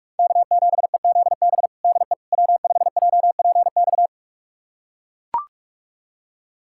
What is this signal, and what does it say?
Morse code, heard as a keyed tone, about 41 wpm, 700 Hz. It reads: K7EZB DE W5JPX
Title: USA Callsign DE Callsign - 5 Characters - 45wpm